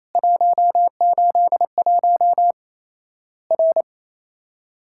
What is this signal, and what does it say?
Morse code, heard as a keyed tone, about 27 wpm, 700 Hz. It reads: 181 R
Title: Worked All Europe DX Contest - QTC Exchange - 28wpm